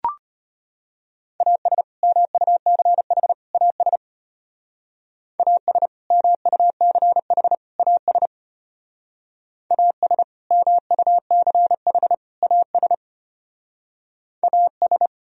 Morse code, heard as a keyed tone, about 32 wpm, 700 Hz. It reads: AS MUCH AS AS MUCH AS AS MUCH AS AS